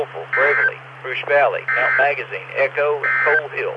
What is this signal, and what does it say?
Emergency Alert System (SAME) tones, heard as harsh buzzing bursts, with speech. It is heard from 0.2 to 3.5 s.